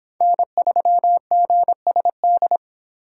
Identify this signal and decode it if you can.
Morse code, heard as a keyed tone, about 26 wpm, 700 Hz. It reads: N3GSD